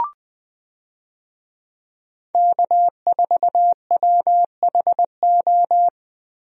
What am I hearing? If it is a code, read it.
Morse code, heard as a keyed tone, about 20 wpm, 700 Hz. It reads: K4WHO
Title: USA Call Signs - 5 Characters - 20wpm